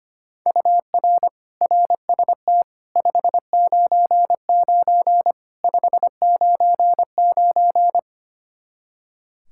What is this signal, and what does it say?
Morse code, heard as a keyed tone, about 25 wpm, 700 Hz. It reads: UR RST 599 599
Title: Practice QSO - Exchange 1 - Beginning - 25wpm